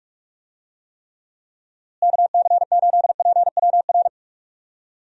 Morse code, heard as a keyed tone, about 46 wpm, 700 Hz. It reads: KC8PWR
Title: USA Call Signs - 6 Characters - Repeated 3x - 45wpm